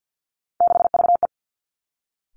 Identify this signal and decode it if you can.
Morse code, heard as a keyed tone, about 50 wpm, 700 Hz. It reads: 6VE
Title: Three Challenging Letter-Number Combinations 50wpm